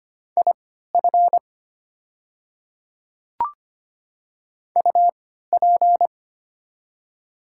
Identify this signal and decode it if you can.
Morse code, heard as a keyed tone, about 25 wpm, spaced as for 14 wpm, 700 Hz. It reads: IF UP
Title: 3x Character Spacing - Top 300 Words - 25wpm